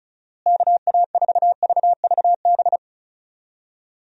Morse code, heard as a keyed tone, about 36 wpm, 700 Hz. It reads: KA4VVB
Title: USA Call Signs - 6 Characters - Repeated 2x - 35wpm